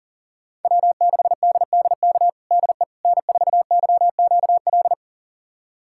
Morse code, heard as a keyed tone, about 40 wpm, 700 Hz. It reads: W6DDK DE N4YQL
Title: USA Callsign DE Callsign - 5 Characters - 40wpm